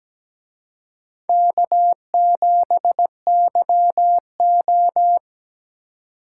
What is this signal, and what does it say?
Morse code, heard as a keyed tone, about 20 wpm, 700 Hz. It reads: K7YO
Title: Speed Racing - USA 1x2 Call Signs 17wpm